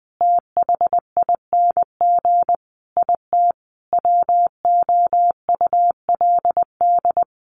Morse code, heard as a keyed tone, about 20 wpm, 700 Hz. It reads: THING IT WOULD
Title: Sentences from Top 1300 Words 20wpm